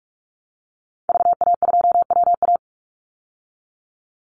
Morse code, heard as a keyed tone, about 44 wpm, 700 Hz. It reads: VA2WA
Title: Call Signs Repeated 3x - 45wpm